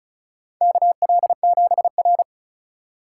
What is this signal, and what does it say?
Morse code, heard as a keyed tone, about 36 wpm, 700 Hz. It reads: KL7R
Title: USA Call Signs - 4x Character Spacing - 35wpm